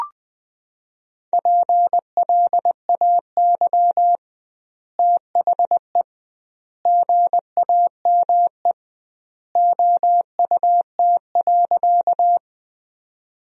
Morse code, heard as a keyed tone, about 20 wpm, 700 Hz. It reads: PLAY THE GAME OUT.
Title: 2x Word-Spacing - Sentences from Top 500 Words - 20wpm